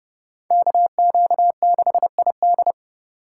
Morse code, heard as a keyed tone, about 38 wpm, 700 Hz. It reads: KQ6ID